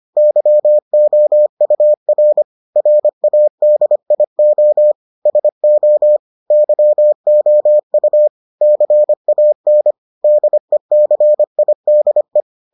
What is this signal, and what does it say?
Morse code, heard as a keyed tone, about 25 wpm, 600 Hz. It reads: YOUR RADIO SO YOU CAN DECIDE